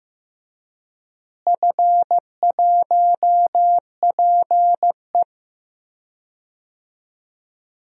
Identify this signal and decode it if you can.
Morse code, heard as a keyed tone, about 15 wpm, 700 Hz. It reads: F1PE
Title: DXCC-100 Call Signs - 4 Characters - Repeated 3x - 15wpm